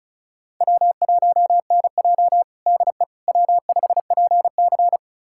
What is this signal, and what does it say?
Morse code, heard as a keyed tone, about 36 wpm, 700 Hz. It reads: W1NJ DE W5PC
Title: USA Callsign DE Callsign - 4 Characters - 35wpm